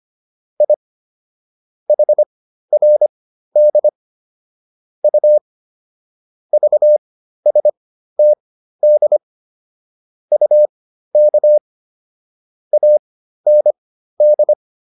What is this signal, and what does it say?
Morse code, heard as a keyed tone, about 25 wpm, spaced as for 13 wpm, 600 Hz. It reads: I HRD U VSTD UK AND